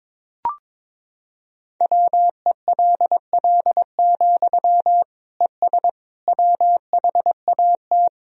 Morse code, heard as a keyed tone, about 22 wpm, 700 Hz. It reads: WELL, ES WHAT
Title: Sentences from Top 200 Words and Abbreviations 22wpm